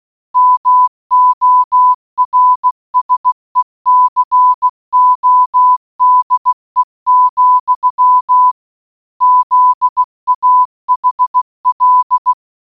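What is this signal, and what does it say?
Morse code, heard as a keyed tone, about 16 wpm, 1000 Hz. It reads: MORSECODE, ZAHL
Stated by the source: Morse audio for a geocache puzzle, GCB8MYB Morse-Bonus Reloaded, in Baden-Württemberg, Germany, created by Wolli.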